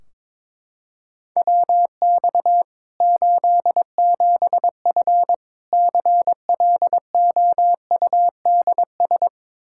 Morse code, heard as a keyed tone, about 22 wpm, 700 Hz. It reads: WX 87F CLOUDS
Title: K1USN SST QSO Practice 22wpm